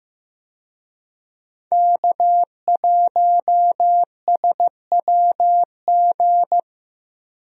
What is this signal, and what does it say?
Morse code, heard as a keyed tone, about 15 wpm, 700 Hz. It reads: K1SWG
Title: USA Call Signs - 5 Characters - 15wpm